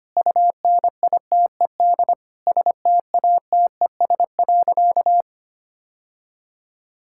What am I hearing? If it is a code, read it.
Morse code, heard as a keyed tone, about 25 wpm, 700 Hz. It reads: UNITED STATES.
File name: practice / 25wpm / Sentences with 2-Word Phrases - 25wpm